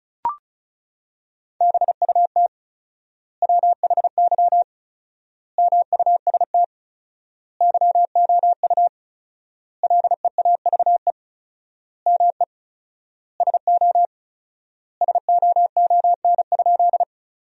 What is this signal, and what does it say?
Morse code, heard as a keyed tone, about 36 wpm, 700 Hz. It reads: BUT WHY MUST YOU LEAVE ME SO SOON?
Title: 4x Word-Spacing - Sentences from Top 400 Words - 35wpm